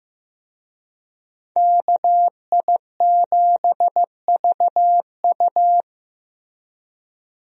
Morse code, heard as a keyed tone, about 15 wpm, 700 Hz. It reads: KI7VU
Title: USA Call Signs - 5 Characters - Repeated 2x - 15wpm